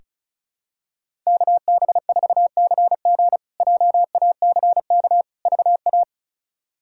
Morse code, heard as a keyed tone, about 36 wpm, 700 Hz. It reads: KB4CG JACK VA
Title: K1USN SST QSO Practice - 35wpm